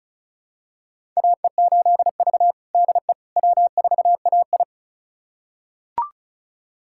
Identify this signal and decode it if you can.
Morse code, heard as a keyed tone, about 36 wpm, 700 Hz. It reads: AE8V DE W4AI
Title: USA Callsign DE Callsign - 4 Characters - 35wpm